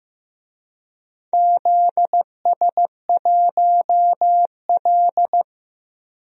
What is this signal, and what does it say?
Morse code, heard as a keyed tone, about 15 wpm, 700 Hz. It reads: ZS1L
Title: DXCC-100 Call Signs - 4 Characters - 15wpm